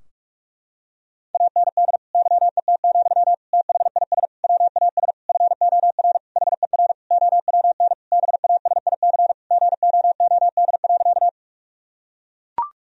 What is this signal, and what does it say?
Morse code, heard as a keyed tone, about 41 wpm, 700 Hz. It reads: AND YET, THIS WAS FOR HER OWN BASIC GOOD.
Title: Sentences from Top 1000 Words - Review Part 3 - 45wpm